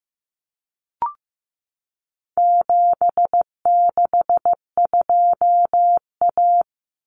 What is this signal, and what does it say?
Morse code, heard as a keyed tone, about 15 wpm, 700 Hz. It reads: 762A
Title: Common Cut Numbers - 4-Digit 15wpm